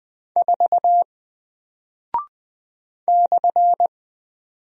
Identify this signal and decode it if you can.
Morse code, heard as a keyed tone, about 20 wpm, 700 Hz. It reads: 4 /
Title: Single Character (Letter:Number:Punctuation:Prosign) Alt 20wpm